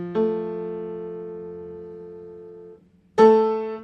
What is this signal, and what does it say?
Dial tone (North America), 350 + 440 Hz.